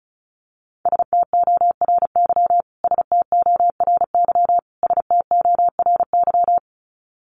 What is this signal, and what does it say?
Morse code, heard as a keyed tone, about 34 wpm, 700 Hz. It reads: STORY STORY STORY
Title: Top 300 Words Repeated 35wpm